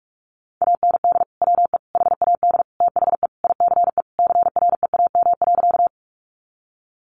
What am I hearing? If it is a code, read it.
Morse code, heard as a keyed tone, about 44 wpm, 700 Hz. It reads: AND WE HAD THE ICE CREAM.
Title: Sentences from Top 1500 Words 45wpm